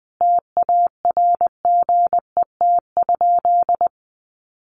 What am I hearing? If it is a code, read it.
Morse code, heard as a keyed tone, about 20 wpm, 700 Hz. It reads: TARGET?
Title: Sentences from Top 1200 Words 20wpm